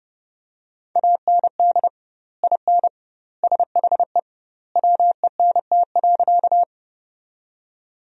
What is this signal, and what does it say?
Morse code, heard as a keyed tone, about 30 wpm, 700 Hz. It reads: AND IN SHE WENT.
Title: 2x Word-Spacing - Sentences from Top 200 Words - 30wpm